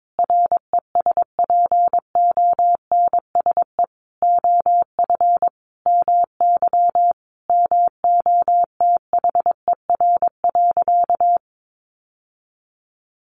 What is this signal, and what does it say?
Morse code, heard as a keyed tone, about 22 wpm, 700 Hz. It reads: RESPONSE OF MY MOTHER.
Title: Sentences from Top 1400 Words 22wpm